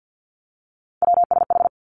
Morse code, heard as a keyed tone, about 50 wpm, 700 Hz. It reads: WSH
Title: Three Challenging Letter-Number Combinations 50wpm